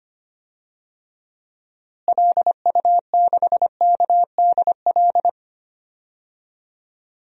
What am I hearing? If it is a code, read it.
Morse code, heard as a keyed tone, about 25 wpm, 700 Hz. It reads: LU6KDL
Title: DXCC-100 Call Signs - 6 Characters - Repeated 3x - 25wpm